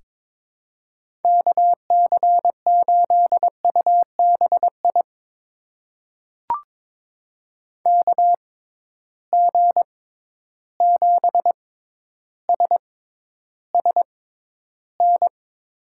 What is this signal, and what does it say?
Morse code, heard as a keyed tone, about 21 wpm, spaced as for 11 wpm, 700 Hz. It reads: KC8UBI KG7SSN
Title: USA Call Signs - 6 Characters - 6x Character Spacing - 22wpm